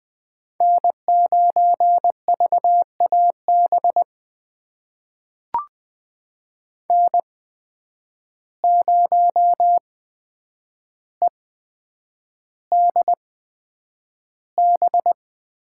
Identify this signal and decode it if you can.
Morse code, heard as a keyed tone, about 20 wpm, spaced as for 9 wpm, 700 Hz. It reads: N9VAB N0EDB
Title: USA Call Signs - 5 Characters - 8x Character Spacing - 20wpm